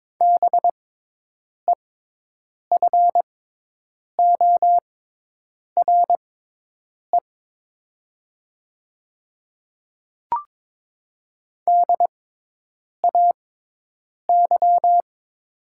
Morse code, heard as a keyed tone, about 22 wpm, spaced as for 8 wpm, 700 Hz. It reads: BEFORE DAY
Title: 6x Character Spacing - Top 200 Words - 22wpm